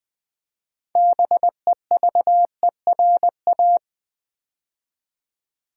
Morse code, heard as a keyed tone, about 20 wpm, 700 Hz. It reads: BEVERA
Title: Build A Word - 2 Letters At A Time - 8-Letter Words - 20wpm